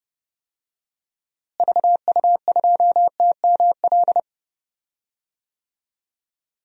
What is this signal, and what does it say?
Morse code, heard as a keyed tone, about 32 wpm, 700 Hz. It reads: VU2TML
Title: DXCC-100 Call Signs - 6 Characters - Repeated 2x - 30wpm